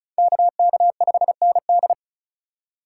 Morse code, heard as a keyed tone, about 36 wpm, 690 Hz. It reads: KK5ND
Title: USA Call Signs - 5 Characters - 3x Character Spacing - 35wpm